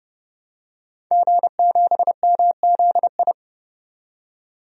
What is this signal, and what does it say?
Morse code, heard as a keyed tone, about 30 wpm, 700 Hz. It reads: G7MZI
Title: DXCC-100 Call Signs - 5 Characters - 30wpm